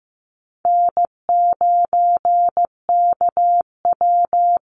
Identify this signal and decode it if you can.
Morse code, heard as a keyed tone, about 15 wpm, 700 Hz. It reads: N9KW
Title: NA SOTA Chaser Callsigns 15wpm